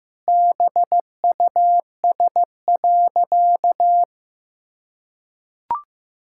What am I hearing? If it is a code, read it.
Morse code, heard as a keyed tone, about 15 wpm, 700 Hz. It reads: BUS.
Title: Sentences from Top 1000 Words - Review Part 6 - 15wpm